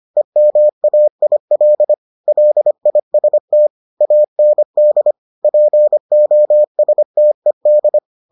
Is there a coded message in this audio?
Yes: Morse code from 0.2 to 8.0 s, heard as a keyed tone, about 25 wpm, 600 Hz. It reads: EMAIL LIST AND POSTED